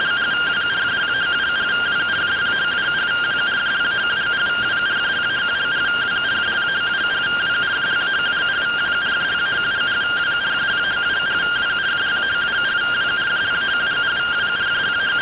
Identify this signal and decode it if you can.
Two-tone teleprinter signal, about 170 Hz shift.
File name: findings / 81 axial iodine 262.gen-31 rtty rx